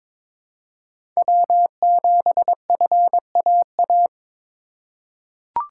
Morse code, heard as a keyed tone, about 22 wpm, 700 Hz. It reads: W7FAA
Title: USA Call Signs - 5 Characters - 4x Character Spacing - 22wpm